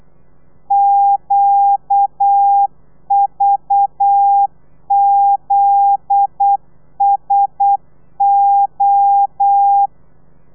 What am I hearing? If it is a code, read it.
Morse code, heard as a keyed tone, about 8 wpm, 800 Hz. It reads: QVZSO
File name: telegrafia / i2tfj / Lez9 35